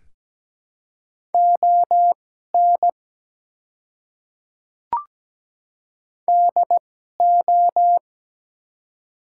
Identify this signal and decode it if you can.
Morse code, heard as a keyed tone, about 17 wpm, spaced as for 12 wpm, 700 Hz. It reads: ON DO
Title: 2x Character Spacing - Top 200 Words - 17wpm